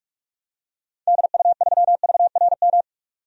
Morse code, heard as a keyed tone, about 46 wpm, 700 Hz. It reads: DU3VRM
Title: DXCC-100 Call Signs - 6 Characters - 45wpm